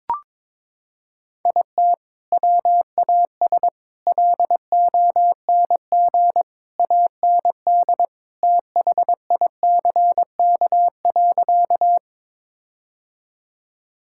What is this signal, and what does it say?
Morse code, heard as a keyed tone, about 22 wpm, 700 Hz. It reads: IT WAS LONG AND THICK.
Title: Sentences from Top 1600 Words 22wpm